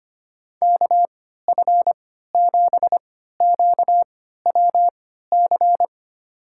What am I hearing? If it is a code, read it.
Morse code, heard as a keyed tone, about 25 wpm, spaced as for 14 wpm, 700 Hz. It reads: KF7QWC
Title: USA Call Signs - 3x Character Spacing - 25wpm